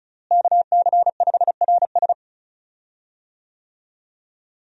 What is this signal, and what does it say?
Morse code, heard as a keyed tone, about 34 wpm, 700 Hz. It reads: KC5RS
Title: USA Call Signs - 5 Characters - Repeated 2x - 35wpm